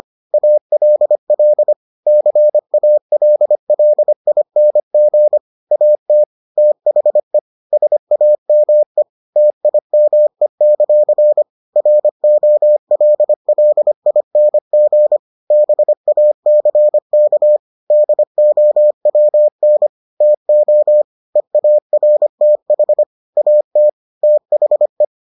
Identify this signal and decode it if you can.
Morse code, heard as a keyed tone, about 25 wpm, 600 Hz. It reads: ALL CALLING AT THE SAME TIME; ROLLING BACK DOWN TO EARTH AT THE